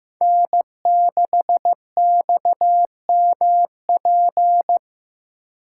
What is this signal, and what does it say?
Morse code, heard as a keyed tone, about 15 wpm, 700 Hz. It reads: N6XMP
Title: USA Call Signs - 3x Character Spacing - 15wpm